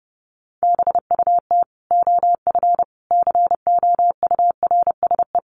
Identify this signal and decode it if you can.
Morse code, heard as a keyed tone, about 30 wpm, 700 Hz. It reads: BUT OF COURSE
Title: Sentences from Top 1500 Words 30wpm